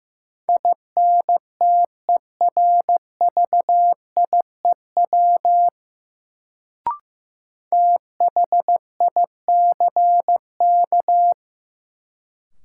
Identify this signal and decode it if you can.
Morse code, heard as a keyed tone, about 15 wpm, 700 Hz. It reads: INTERVIEW THICK
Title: Sentences from Top 1600 Words - New Word Drill - 15wpm